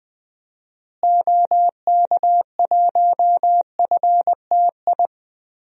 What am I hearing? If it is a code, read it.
Morse code, heard as a keyed tone, about 20 wpm, 700 Hz. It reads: OK1FTI
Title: DXCC-100 Call Signs - 6 Characters - Repeated 2x - 20wpm